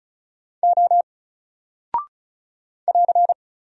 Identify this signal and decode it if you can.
Morse code, heard as a keyed tone, about 35 wpm, 700 Hz. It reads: O <AR>
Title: Single Character (Letter:Number:Punctuation:Prosign) 35wpm